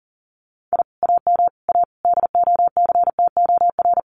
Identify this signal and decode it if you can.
Morse code, heard as a keyed tone, about 37 wpm, 700 Hz. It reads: I AM A DOCTOR
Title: Common QSO Phrases Encore 40wpm